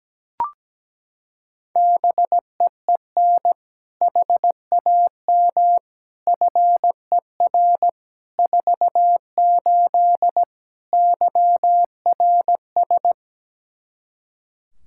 Morse code, heard as a keyed tone, about 17 wpm, 700 Hz. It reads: BEEN HAM FER 48 YRS
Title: Practice QSO - Exchange 3 - More Info - 17wpm